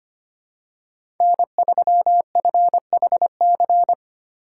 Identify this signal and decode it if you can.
Morse code, heard as a keyed tone, about 25 wpm, 700 Hz. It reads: N3FHC
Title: USA Call Signs - 5 Characters - 2x Character Spacing - 25wpm